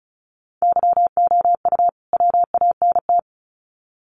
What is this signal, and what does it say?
Morse code, heard as a keyed tone, about 34 wpm, 700 Hz. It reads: YOU WANT
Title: Sets of 2 Words 35wpm